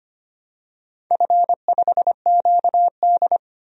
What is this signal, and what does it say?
Morse code, heard as a keyed tone, about 25 wpm, 690 Hz. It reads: F5QD
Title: DXCC-100 Call Signs - 4 Characters - 25wpm